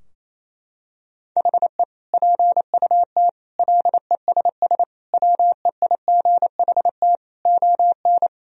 Morse code, heard as a keyed tone, about 28 wpm, 700 Hz. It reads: HE PUT LESS WEIGHT ON
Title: Sentences from Top 1000 Words - Review Part 3 - 28wpm